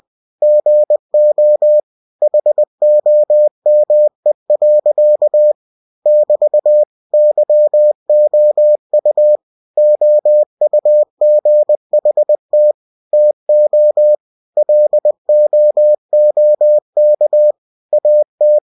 Morse code, heard as a keyed tone, about 20 wpm, 600 Hz. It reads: GO HOME. <BT> YOU OUGHT TO LOOK AT
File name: cw-books / winnie-the-pooh / winnie-20wpm-page-0069